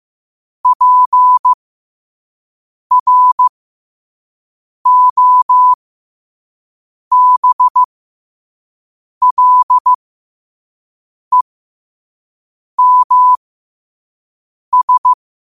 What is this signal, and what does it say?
Morse code, heard as a keyed tone, about 15 wpm, spaced as for 5 wpm, 1000 Hz. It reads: PROBLEMS
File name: qotdfiles / 5wpm